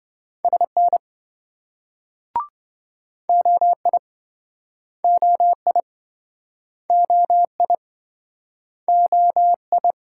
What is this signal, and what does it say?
Morse code, heard as a keyed tone, about 27 wpm, 700 Hz. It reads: SN OI OI OI OI